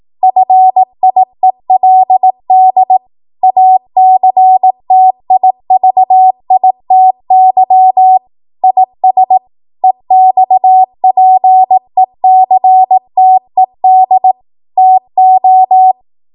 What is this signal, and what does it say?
Morse code, heard as a keyed tone, about 18 wpm, 750 Hz. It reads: FIELD ACTIVITY IS EXPECTED TO
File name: Morse / 18WPM